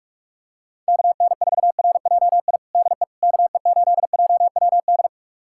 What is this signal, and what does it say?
Morse code, heard as a keyed tone, about 46 wpm, 700 Hz. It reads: KN4RJI DE KE8JWD
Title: USA Callsign DE Callsign - 6 Characters - 45wpm